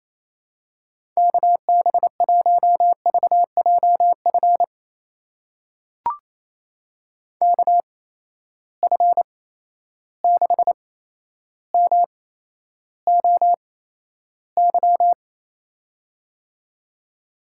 Morse code, heard as a keyed tone, about 27 wpm, spaced as for 12 wpm, 700 Hz. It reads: KB1VJF KF6MOY
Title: USA Call Signs - 6 Characters - 8x Character Spacing - 28wpm